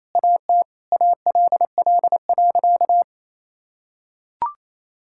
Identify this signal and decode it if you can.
Morse code, heard as a keyed tone, about 27 wpm, 700 Hz. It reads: AT ALL.
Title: Sentences from Top 1600 Words 28wpm